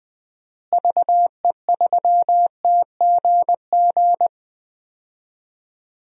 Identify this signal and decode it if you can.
Morse code, heard as a keyed tone, about 20 wpm, 700 Hz. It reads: VE3TGG
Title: DXCC-100 Call Signs - 6 Characters - Repeated 3x - 20wpm